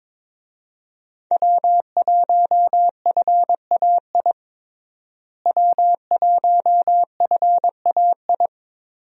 Morse code, heard as a keyed tone, about 22 wpm, 700 Hz. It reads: W1FAI W1FAI
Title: USA Call Signs - 5 Characters - Repeated 2x - 22wpm